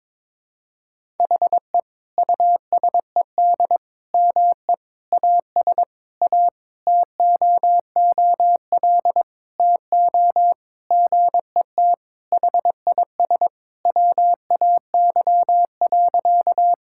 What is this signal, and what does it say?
Morse code, heard as a keyed tone, about 22 wpm, 700 Hz. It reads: HE USED ME AS A TOOL TO GET HIS WAY.
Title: Sentences from Top 1000 Words - Review Part 6 - 22wpm